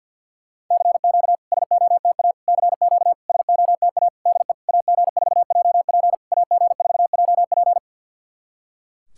Morse code, heard as a keyed tone, about 52 wpm, 690 Hz. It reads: CQ SOTA CQ SOTA DE AG4JP AG4JP
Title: SOTA QSO Practice 50wpm